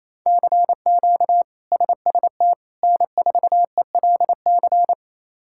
Morse code, heard as a keyed tone, about 28 wpm, 700 Hz. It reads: CQ SST N4ELC
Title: K1USN SST QSO Practice - 28wpm